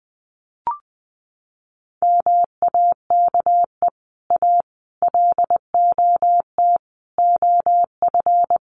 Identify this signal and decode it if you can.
Morse code, heard as a keyed tone, about 20 wpm, 700 Hz. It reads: MAKE A LOT OF
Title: Sets of 5 Words Encore2 20wpm